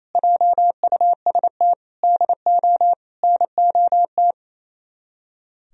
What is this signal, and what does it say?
Morse code, heard as a keyed tone, about 35 wpm, 700 Hz. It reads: JUST DO NOT